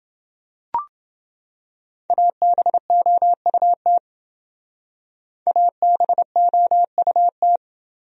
Morse code, heard as a keyed tone, about 27 wpm, 700 Hz. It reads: ABOUT ABOUT